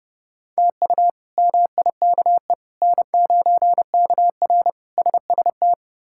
Morse code, heard as a keyed tone, about 30 wpm, 700 Hz. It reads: TU MIKE N9KR SST
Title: K1USN SST QSO Practice - 30wpm